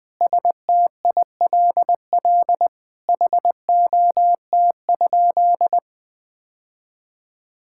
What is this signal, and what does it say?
Morse code, heard as a keyed tone, about 20 wpm, 700 Hz. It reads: STILL HOT?
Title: Sentences from Top 1000 Words - Review Part 6 - 20wpm